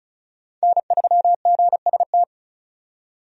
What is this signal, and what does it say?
Morse code, heard as a keyed tone, about 36 wpm, 700 Hz. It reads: N3GST
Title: USA Call Signs - 5 Characters - 35wpm